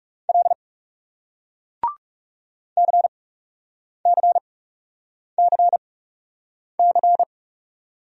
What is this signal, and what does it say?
Morse code, heard as a keyed tone, about 40 wpm, spaced as for 8 wpm, 700 Hz. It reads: L CCCC